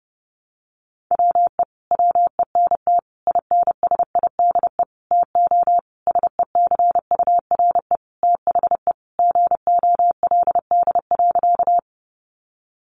Morse code, heard as a keyed tone, about 30 wpm, 700 Hz. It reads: WE WENT INSIDE TO SECURE THE GOLD.
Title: Sentences from Top 1200 Words 30wpm